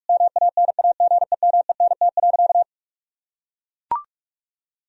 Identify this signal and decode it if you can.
Morse code, heard as a keyed tone, about 41 wpm, 700 Hz. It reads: MANAGEMENT.
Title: Sentences from Top 1000 Words - Review Part 4 - 45wpm